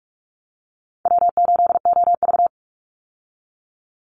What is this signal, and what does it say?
Morse code, heard as a keyed tone, about 44 wpm, 700 Hz. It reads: W8OV
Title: Easy Call Signs Repeated 3x - 45wpm